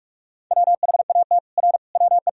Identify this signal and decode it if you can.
Morse code, heard as a keyed tone, about 47 wpm, 700 Hz. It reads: WHAT R WE
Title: Sentences from Top 100 Words and Abbreviations 45wpm